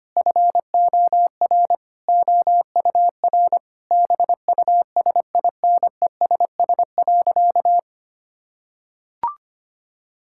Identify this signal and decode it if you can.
Morse code, heard as a keyed tone, about 25 wpm, 700 Hz. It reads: FOR OUR BUSINESS.
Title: Sentences from Top 1000 Words - Review Part 6 - 25wpm